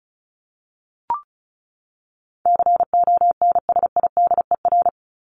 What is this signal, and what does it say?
Morse code, heard as a keyed tone, about 34 wpm, 700 Hz. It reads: CONSIDER
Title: ING Suffix - Top 100 Words with Base Words 35wpm